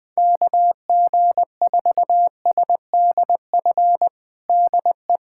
Morse code, heard as a keyed tone, about 21 wpm, 700 Hz. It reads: KG4SDF DE
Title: USA Callsign DE Callsign - 6 Characters - 20wpm